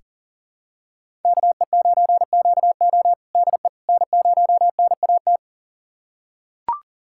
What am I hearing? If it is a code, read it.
Morse code, heard as a keyed tone, about 40 wpm, 700 Hz. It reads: KE9QO DE N0NAT
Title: USA Callsign DE Callsign - 5 Characters - 40wpm